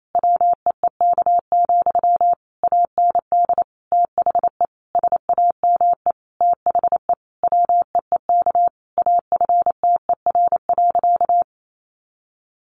Morse code, heard as a keyed tone, about 28 wpm, 700 Hz. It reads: WEEK, AND THE SAME THE WEEK AFTER.
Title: Sentences from Top 500 Words 28wpm